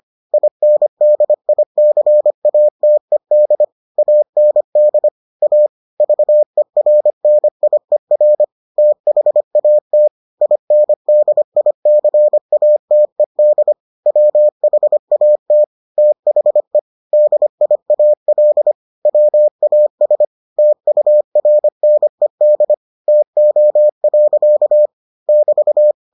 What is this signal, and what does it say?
Morse code, heard as a keyed tone, about 25 wpm, 600 Hz. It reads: INDICATED AND A VERNIER THAT INDICATED WHAT THE DIAL WAS TURNED TO. <BT>